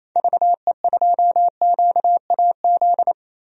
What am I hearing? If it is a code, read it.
Morse code, heard as a keyed tone, about 28 wpm, 700 Hz. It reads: VE2QAZ
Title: DXCC-100 Call Signs - 6 Characters - Repeated 2x - 28wpm